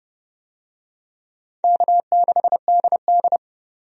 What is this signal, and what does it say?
Morse code, heard as a keyed tone, about 30 wpm, 700 Hz. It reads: K6DD
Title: USA Call Signs - 4 Characters - 30wpm